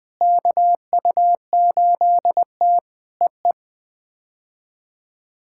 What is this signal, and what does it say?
Morse code, heard as a keyed tone, about 20 wpm, 700 Hz. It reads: KU8T EE
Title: POTA QSO Practice - 20wpm